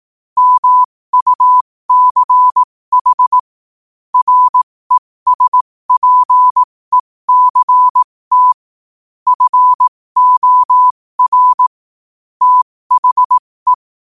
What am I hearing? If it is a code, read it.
Morse code, heard as a keyed tone, about 18 wpm, spaced as for 15 wpm, 1000 Hz. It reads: MUCH RESPECT FOR THE
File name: qotdfiles / 15wpm